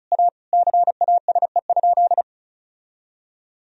Morse code, heard as a keyed tone, about 34 wpm, 700 Hz. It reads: A CASE?
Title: Sentences from Top 1000 Words - Review Part 2 - 35wpm